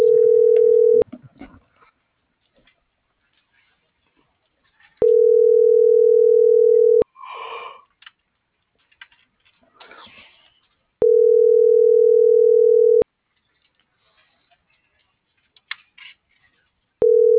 Ringback tone (North America), 440 + 480 Hz.